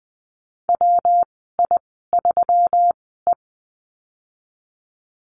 Morse code, heard as a keyed tone, about 20 wpm, spaced as for 15 wpm, 700 Hz. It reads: WI3E
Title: 193 Lesson 31g - Letter v - Call Signs 20wpm